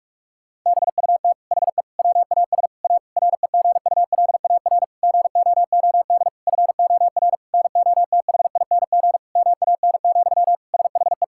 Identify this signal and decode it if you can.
Morse code, heard as a keyed tone, about 46 wpm, 700 Hz. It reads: BUT HE WAS A REGULAR GOOD FOR NOTHING MAN, SHE
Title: Sentences from Top 1000 Words - Review Part 6 - 45wpm